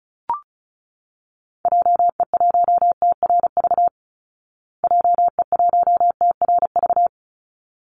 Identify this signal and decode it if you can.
Morse code, heard as a keyed tone, about 34 wpm, 700 Hz. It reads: JE1TRV JE1TRV
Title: Call Signs Repeated 3x - 35wpm